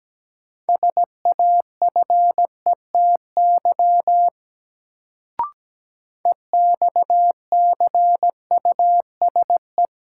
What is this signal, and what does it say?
Morse code, heard as a keyed tone, about 17 wpm, 700 Hz. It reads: SAFETY EXCUSE
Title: Sentences from Top 700 Words - New Word Drill - 17wpm